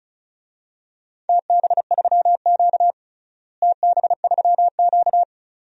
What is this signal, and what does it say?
Morse code, heard as a keyed tone, about 34 wpm, 700 Hz. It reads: TB3Q TB3Q
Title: DXCC-100 Call Signs - 4 Characters - Repeated 2x - 35wpm